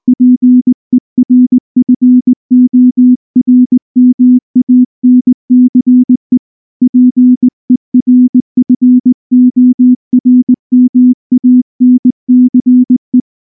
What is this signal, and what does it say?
Morse code, heard as a keyed tone, about 19 wpm, 250 Hz. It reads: PERFORMANCE PERFORMANCE
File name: morse-training-20WPM